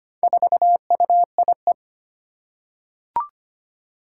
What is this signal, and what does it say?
Morse code, heard as a keyed tone, about 25 wpm, 700 Hz. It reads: 4UIE
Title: Four Challenging Letter-Number Combinations 25wpm